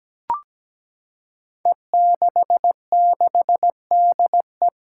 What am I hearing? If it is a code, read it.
Morse code, heard as a keyed tone, about 17 wpm, 700 Hz. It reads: E66DE
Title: Five Challenging Letter-Number Combinations 17wpm